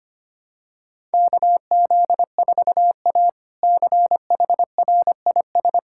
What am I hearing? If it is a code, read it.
Morse code, heard as a keyed tone, about 25 wpm, 700 Hz. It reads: KZ4A CHRIS
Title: K1USN SST QSO Practice - 25wpm